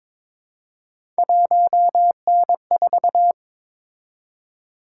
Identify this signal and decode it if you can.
Morse code, heard as a keyed tone, about 22 wpm, 700 Hz. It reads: 1N4
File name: practice / 22wpm / Three Letter-Number Combinations - 22wpm